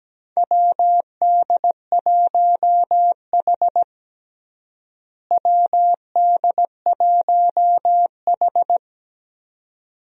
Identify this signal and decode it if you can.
Morse code, heard as a keyed tone, about 17 wpm, 700 Hz. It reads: WD1H WD1H
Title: USA Call Signs - 4 Characters - Repeated 2x - 17wpm